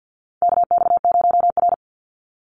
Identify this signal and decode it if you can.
Morse code, heard as a keyed tone, about 50 wpm, 700 Hz. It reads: KX0R
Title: NA SOTA Chaser Callsigns 50wpm